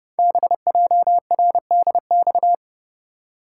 Morse code, heard as a keyed tone, about 30 wpm, 700 Hz. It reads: BJRDX
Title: Code Groups - 5 Letters - 30wpm